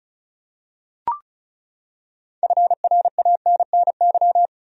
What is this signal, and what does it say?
Morse code, heard as a keyed tone, about 34 wpm, 700 Hz. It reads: FRANNY
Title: USA Common Names - 35wpm